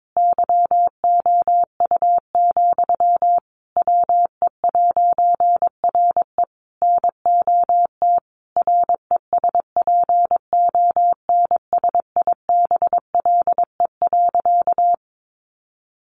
Morse code, heard as a keyed tone, about 22 wpm, 700 Hz. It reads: YOU, WE'RE NOT RESPONSIBLE.
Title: A Sound Of Thunder 22wpm